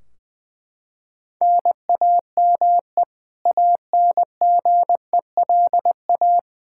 Morse code, heard as a keyed tone, about 20 wpm, 700 Hz. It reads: NAME ANGELA
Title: K1USN SST QSO Practice 20wpm